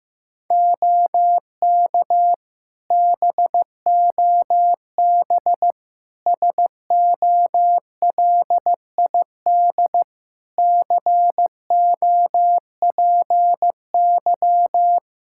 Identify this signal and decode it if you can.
Morse code, heard as a keyed tone, about 15 wpm, 700 Hz. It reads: OK BOB SOLID COPY
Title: Practice QSO - Exchange 3 - More Info - 15wpm